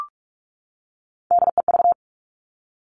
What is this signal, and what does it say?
Morse code, heard as a keyed tone, about 46 wpm, 700 Hz. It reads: DEV
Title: Three Challenging Letter-Number Combinations 45wpm